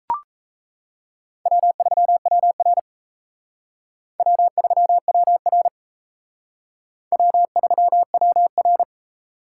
Morse code, heard as a keyed tone, about 36 wpm, 700 Hz. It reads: W3WR W3WR W3WR